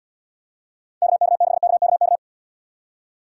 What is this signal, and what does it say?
Morse code, heard as a keyed tone, about 77 wpm, 700 Hz. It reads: KK6KKK